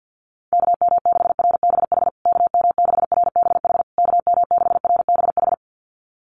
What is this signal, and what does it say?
Morse code, heard as a keyed tone, about 50 wpm, 700 Hz. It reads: KM6RBH KM6RBH KM6RBH
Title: USA Call Signs Repeated 3x 50wpm